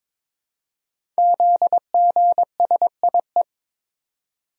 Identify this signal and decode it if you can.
Morse code, heard as a keyed tone, about 23 wpm, 700 Hz. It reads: ZGSIE
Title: Code Groups - 5 Letters - 22wpm